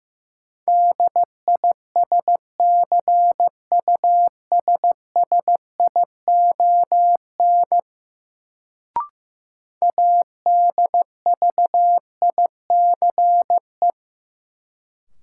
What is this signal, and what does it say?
Morse code, heard as a keyed tone, about 16 wpm, 700 Hz. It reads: DISCUSSION ADVICE
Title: Sentences from Top 1100 Words - New Word Drill - 15wpm